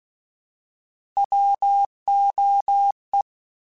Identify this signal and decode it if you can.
Morse code, heard as a keyed tone, about 16 wpm, 800 Hz. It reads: WOE